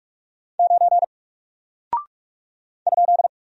Morse code, heard as a keyed tone, about 46 wpm, 700 Hz. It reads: 9 ?
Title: Single Character (Letter:Number:Punctuation:Prosign) Alt 45wpm